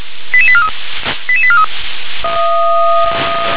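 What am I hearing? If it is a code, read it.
Five-tone selective call (ZVEI), heard as quick runs of five tones, tones of about 70 ms. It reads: 88843 88843
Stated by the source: ZVEI five-tone call received on an SDR.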